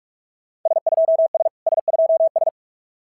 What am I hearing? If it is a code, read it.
Morse code, heard as a keyed tone, about 44 wpm, 690 Hz. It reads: S2S S2S
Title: SOTA QSO Practice 45wpm